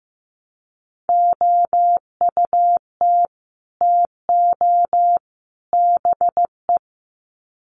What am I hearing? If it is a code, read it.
Morse code, heard as a keyed tone, about 15 wpm, 700 Hz. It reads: OUT TO BE
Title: Sets of 3 Words 15wpm